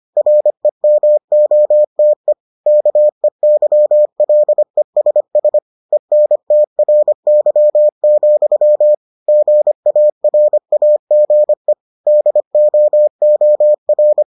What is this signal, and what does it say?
Morse code, heard as a keyed tone, about 25 wpm, 600 Hz. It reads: REMOTE KEYLESS ENTRY, GARAGE DOOR